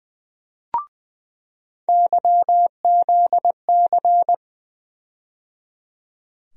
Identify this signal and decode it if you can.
Morse code, heard as a keyed tone, about 21 wpm, 700 Hz. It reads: YZC
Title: USA Call Sign Postfixes - 3 Letters 20wpm